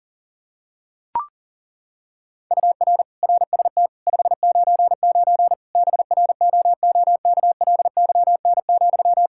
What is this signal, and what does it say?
Morse code, heard as a keyed tone, about 40 wpm, 700 Hz. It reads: UR RST 599 BROOKLYN,
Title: Practice QSO - Exchange 1 - Beginning - 40wpm